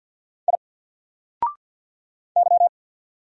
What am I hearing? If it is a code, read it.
Morse code, heard as a keyed tone, about 48 wpm, 700 Hz. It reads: I Y
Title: Single Character (Letter:Number:Punctuation:Prosign) Alt 50wpm